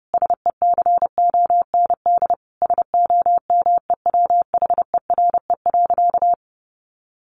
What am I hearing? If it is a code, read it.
Morse code, heard as a keyed tone, about 30 wpm, 700 Hz. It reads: SECOND SOMEWHERE.
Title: Sentences from Top 1400 Words 30wpm